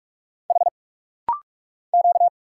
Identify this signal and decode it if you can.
Morse code, heard as a keyed tone, about 46 wpm, 700 Hz. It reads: H Q
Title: Single Letter-Number Alt - Rapid-Fire 45wpm